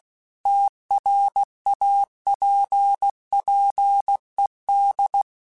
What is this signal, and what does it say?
Morse code, heard as a keyed tone, about 16 wpm, 800 Hz. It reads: TRAPPED